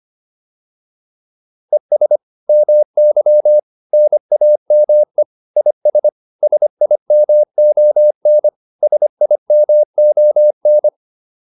Morse code, heard as a keyed tone, about 25 wpm, 600 Hz. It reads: ES MY NAME IS SIMON SIMON